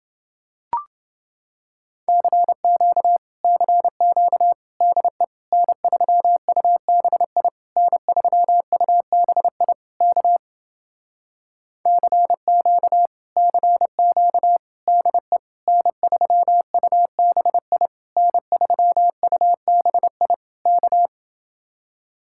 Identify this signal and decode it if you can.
Morse code, heard as a keyed tone, about 30 wpm, 700 Hz. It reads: CQ CQ DE N3UBI N3UBI K CQ CQ DE N3UBI N3UBI K